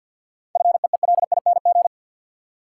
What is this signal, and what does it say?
Morse code, heard as a keyed tone, about 53 wpm, 700 Hz. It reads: FEELING
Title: Sentences from Top 1000 Words - Review Part 3 - New Word Drill - 50wpm